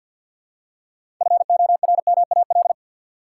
Morse code, heard as a keyed tone, about 53 wpm, 700 Hz. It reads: FORMAL
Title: Six Letter Words - 50wpm